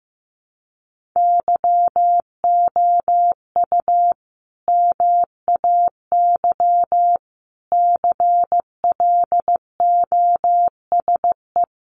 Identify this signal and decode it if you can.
Morse code, heard as a keyed tone, about 15 wpm, 700 Hz. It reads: YOU MAY CLOSE
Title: Sentences from Top 1500 Words 15wpm